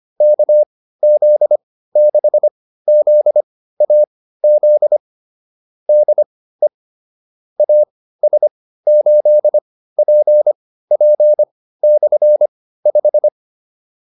Morse code, heard as a keyed tone, about 25 wpm, spaced as for 15 wpm, 600 Hz. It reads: KZ6ZAZ DE AS8PP/5
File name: QSO 302 15